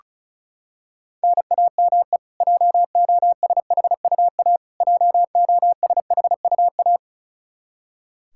Morse code, heard as a keyed tone, about 36 wpm, 700 Hz. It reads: NAME JOSHUA JOSHUA
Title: Practice QSO - Exchange 1 - Beginning - 35wpm